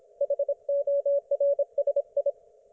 Morse code, heard as a keyed tone, about 27 wpm, 560 Hz, with radio noise. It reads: HORSI